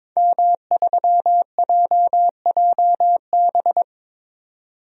Morse code, heard as a keyed tone, about 22 wpm, 700 Hz. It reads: M3JJB
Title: DXCC-100 Call Signs - 5 Characters - 22wpm